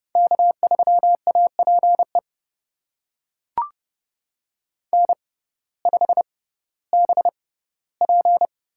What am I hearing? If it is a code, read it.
Morse code, heard as a keyed tone, about 30 wpm, spaced as for 17 wpm, 700 Hz. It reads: K3APE N5BP
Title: USA Call Signs - 5 Characters - 6x Character Spacing - 30wpm